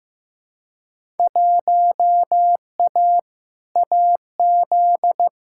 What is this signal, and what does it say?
Morse code, heard as a keyed tone, about 15 wpm, 700 Hz. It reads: 1A AZ
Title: Field Day Exchanges - 15wpm